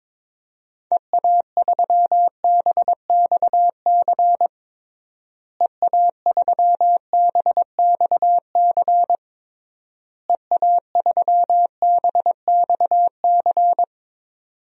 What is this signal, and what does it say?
Morse code, heard as a keyed tone, about 22 wpm, 700 Hz. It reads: EA3BXC EA3BXC EA3BXC
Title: DXCC-100 Call Signs - 6 Characters - Repeated 3x - 22wpm